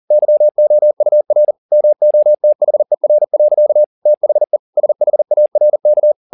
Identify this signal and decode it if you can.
Morse code, heard as a keyed tone, about 40 wpm, 600 Hz. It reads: YOUR MOTHER. THE SHARK
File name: the old man and the sea 2 40wpm